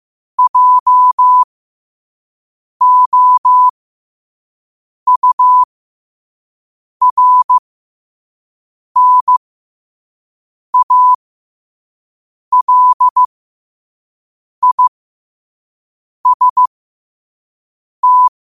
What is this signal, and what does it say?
Morse code, heard as a keyed tone, about 15 wpm, spaced as for 5 wpm, 1000 Hz. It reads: JOURNALIST
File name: qotdfiles / 5wpm